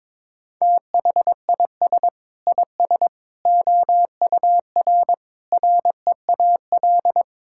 Morse code, heard as a keyed tone, about 22 wpm, 700 Hz. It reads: THIS IS OUR REAL
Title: Sentences from Top 1000 Words - Review Part 5 - 22wpm